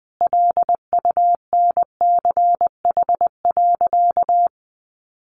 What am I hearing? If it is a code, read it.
Morse code, heard as a keyed tone, about 20 wpm, 700 Hz. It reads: LUNCH.
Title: Easy Phrases 20wpm